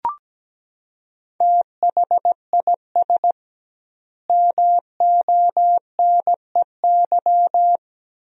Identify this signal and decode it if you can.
Morse code, heard as a keyed tone, about 17 wpm, 700 Hz. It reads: THIS MONEY
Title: 2x Word-Spacing - Sentences from Top 400 Words - 17wpm